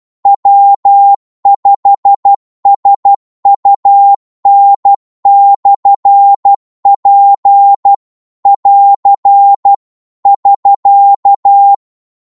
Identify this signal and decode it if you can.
Morse code, heard as a keyed tone, about 12 wpm, 800 Hz. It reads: W5SUN/P <AR> <SK>
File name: Morse / qso-12wpm